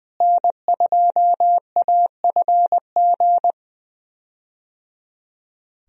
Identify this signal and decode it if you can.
Morse code, heard as a keyed tone, about 21 wpm, 700 Hz. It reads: N2AFG
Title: USA Call Signs - 5 Characters - Repeated 3x - 20wpm